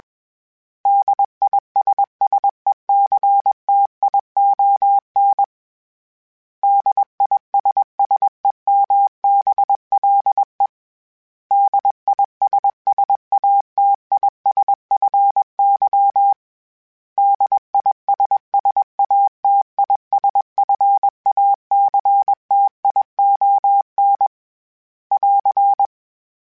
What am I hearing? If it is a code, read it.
Morse code, heard as a keyed tone, about 21 wpm, 800 Hz. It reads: DISSECTION DISSEMBLE DISSATISFY DISSATISFACTION <AR>
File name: General / General-mp3s / Suffixes @20